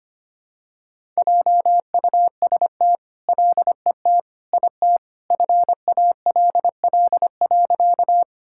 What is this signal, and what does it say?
Morse code, heard as a keyed tone, about 25 wpm, 700 Hz. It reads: JUST LET IT FALL.
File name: practice / 25wpm / All-Age Appropriate Jokes - 25wpm